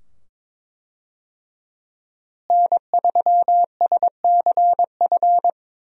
Morse code, heard as a keyed tone, about 21 wpm, 700 Hz. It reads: N3SCF
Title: USA Call Signs - 5 Characters - Repeated 2x - 22wpm